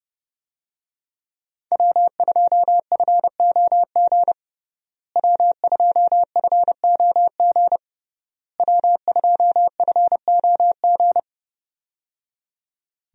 Morse code, heard as a keyed tone, about 30 wpm, 700 Hz. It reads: W2FOG W2FOG W2FOG
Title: USA Call Signs - 5 Characters - Repeated 3x - 30wpm